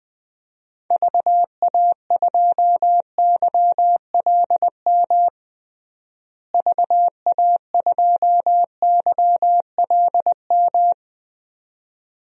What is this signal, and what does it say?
Morse code, heard as a keyed tone, about 20 wpm, 700 Hz. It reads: VA2YLM VA2YLM
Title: DXCC-100 Call Signs - 6 Characters - Repeated 2x - 20wpm